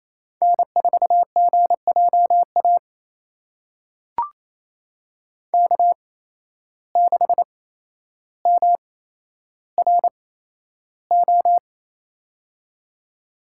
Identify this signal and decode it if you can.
Morse code, heard as a keyed tone, about 28 wpm, spaced as for 12 wpm, 700 Hz. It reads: N4GJA K6MRO
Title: USA Call Signs - 5 Characters - 8x Character Spacing - 28wpm